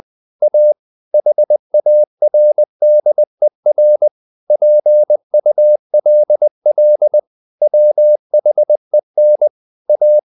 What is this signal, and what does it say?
Morse code, heard as a keyed tone, about 20 wpm, 600 Hz. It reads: A HARDER PULL WHEN A
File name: the old man and the sea / the old man and the sea part 1 20wpm